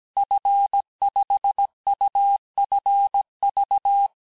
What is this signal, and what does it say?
Morse code, heard as a keyed tone, about 17 wpm, 800 Hz. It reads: F5UFV
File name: cours-du-lundi files / cours-1587-FIN-de-QSO-vitesse-16MPM